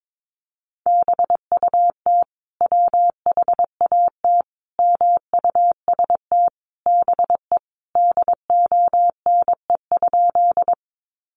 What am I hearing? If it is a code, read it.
Morse code, heard as a keyed tone, about 22 wpm, 700 Hz. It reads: BUT WHAT MUST BE DONE?
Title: Sentences from Top 500 Words 22wpm